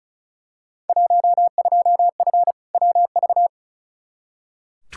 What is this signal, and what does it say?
Morse code, heard as a keyed tone, about 36 wpm, 700 Hz. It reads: 12F WV
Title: Field Day Exchanges Encore - 35wpm